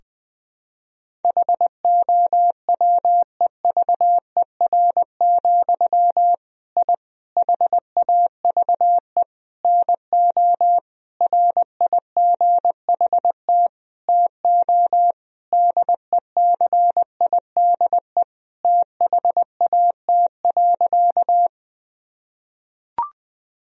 Morse code, heard as a keyed tone, about 20 wpm, 700 Hz. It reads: HOWEVER, I HAVE NO RIGHT TO DECIDE THAT.
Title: Sentences from Top 1000 Words - Review Part 3 - 20wpm